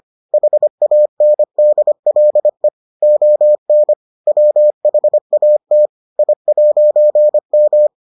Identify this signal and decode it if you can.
Morse code, heard as a keyed tone, about 25 wpm, 600 Hz. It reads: HANDLE ON WHAT I'M